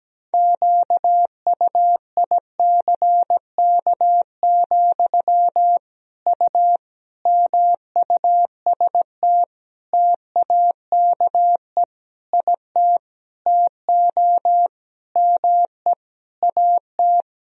Morse code, heard as a keyed tone, about 17 wpm, 700 Hz. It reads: QUICK, U MUST TAKE IT TO ME AT
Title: Sentences from Top 500 Words and Abbreviations 17wpm